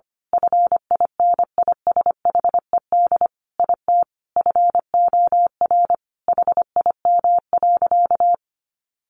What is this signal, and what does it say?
Morse code, heard as a keyed tone, about 25 wpm, 700 Hz. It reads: FINISHED IT FOR HIM.
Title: A Sound Of Thunder 25wpm